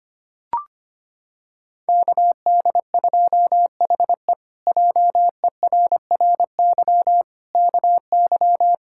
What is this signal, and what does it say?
Morse code, heard as a keyed tone, about 25 wpm, 700 Hz. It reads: KD2HE JERRY KY
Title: K1USN SST QSO Practice - 25wpm